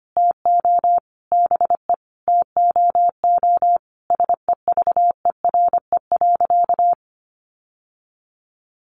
Morse code, heard as a keyed tone, about 25 wpm, 700 Hz. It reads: TO BE TOO SEVERE.
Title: Sentences from Top 1400 Words 25wpm